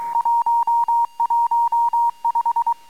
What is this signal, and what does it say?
Morse code, heard as a keyed tone, about 23 wpm, 910 Hz. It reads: 115
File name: amsat / cw